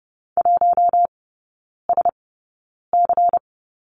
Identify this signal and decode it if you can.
Morse code, heard as a keyed tone, about 30 wpm, spaced as for 9 wpm, 690 Hz. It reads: 1SC
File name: Lesson 17e - Letter u - Three Characters 30wpm